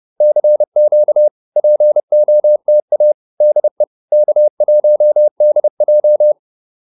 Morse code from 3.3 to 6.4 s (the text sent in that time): DE K1DJ